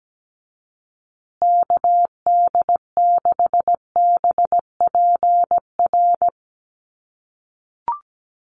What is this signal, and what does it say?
Morse code, heard as a keyed tone, about 17 wpm, 700 Hz. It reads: KD6BPR
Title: USA Call Signs Repeated 2x 17wpm